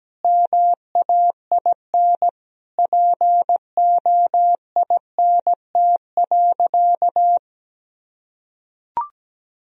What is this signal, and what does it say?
Morse code, heard as a keyed tone, about 17 wpm, 700 Hz. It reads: MAIN POINT.
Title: Sentences from Top 1000 Words - Review Part 2 - 17wpm